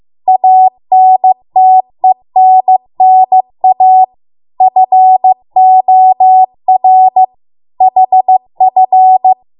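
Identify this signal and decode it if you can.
Morse code, heard as a keyed tone, about 15 wpm, 750 Hz. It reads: ANTENNA FOR HF
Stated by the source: Morse code practice at 15 wpm from the national amateur radio association's code archive.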